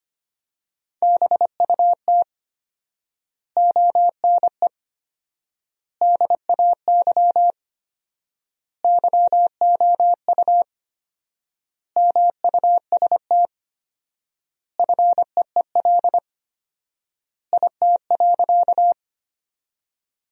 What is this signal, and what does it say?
Morse code, heard as a keyed tone, about 25 wpm, 700 Hz. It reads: BUT ONE DAY YOU MUST FEEL IT.
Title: 4x Word-Spacing - Sentences from Top 400 Words - 25wpm